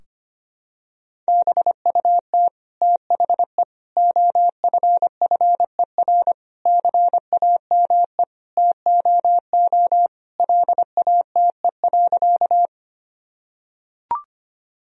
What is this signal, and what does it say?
Morse code, heard as a keyed tone, about 25 wpm, 700 Hz. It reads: BUT THE OFFER CAME TOO LATE.
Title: Sentences from Top 1000 Words - Review Part 4 - 25wpm